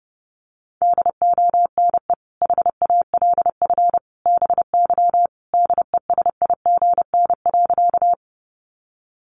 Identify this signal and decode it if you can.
Morse code, heard as a keyed tone, about 30 wpm, 700 Hz. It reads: DONE HALF BY DESIGN.
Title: Sentences from Top 800 Words 30wpm